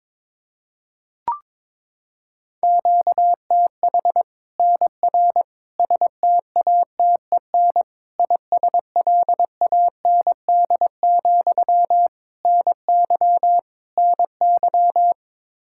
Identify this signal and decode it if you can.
Morse code, heard as a keyed tone, about 22 wpm, 700 Hz. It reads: QTH NR STATEN ISLAND, NY NY
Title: Practice QSO - Exchange 1 - Beginning - 22wpm